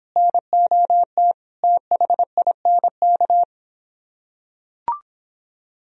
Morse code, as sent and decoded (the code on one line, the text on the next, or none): -. --- - / - .... .. -. -.-
NOT THINK